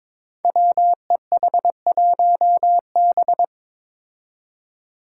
Morse code, heard as a keyed tone, about 22 wpm, 700 Hz. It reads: WEH1B
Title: Five Challenging Letter-Number Combinations 22wpm